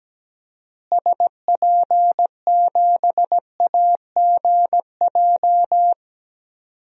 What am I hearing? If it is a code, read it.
Morse code, heard as a keyed tone, about 17 wpm, 700 Hz. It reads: SP7AGJ
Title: DXCC-100 Call Signs - 6 Characters - Repeated 2x - 17wpm